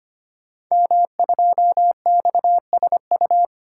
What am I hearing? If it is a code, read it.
Morse code, heard as a keyed tone, about 25 wpm, 700 Hz. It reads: M2XSU
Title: Code Groups - 5 Letter:Numbers - 25wpm